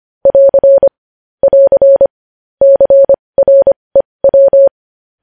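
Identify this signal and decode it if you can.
Morse code, heard as a keyed tone, about 25 wpm, spaced as for 20 wpm, 550 Hz. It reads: <AR> <AR> CREW
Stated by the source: Morse code news headlines practice at 20 wpm.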